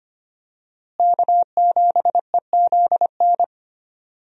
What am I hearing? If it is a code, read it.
Morse code, heard as a keyed tone, about 25 wpm, 700 Hz. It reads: K7EZN
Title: USA Call Signs - 5 Characters - Repeated 2x - 25wpm